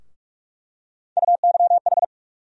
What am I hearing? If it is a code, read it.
Morse code, heard as a keyed tone, about 46 wpm, 700 Hz. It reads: UYH
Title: USA Call Sign Postfixes - 3 Letters 45wpm